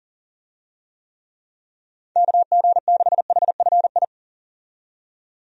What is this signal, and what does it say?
Morse code, heard as a keyed tone, about 39 wpm, 700 Hz. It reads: KG6HFI
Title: USA Call Signs - 6 Characters - 40wpm